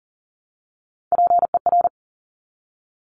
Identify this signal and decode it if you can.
Morse code, heard as a keyed tone, about 38 wpm, 700 Hz. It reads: PER
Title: Top Three Letter Combinations in Words 40wpm